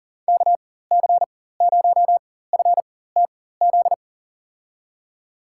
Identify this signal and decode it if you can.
Morse code, heard as a keyed tone, about 40 wpm, spaced as for 19 wpm, 700 Hz. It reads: KC0FTZ
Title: USA Call Signs - 6 Characters - 4x Character Spacing - 40wpm